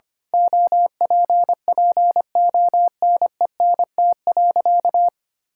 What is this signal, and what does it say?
Morse code, heard as a keyed tone, about 25 wpm, 700 Hz. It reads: OPPONENT.
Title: Pangram Challenge v2 - 25wpm